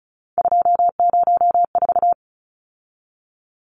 Morse code, heard as a keyed tone, about 34 wpm, 700 Hz. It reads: 204
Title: Numbers - 3-Digit 35wpm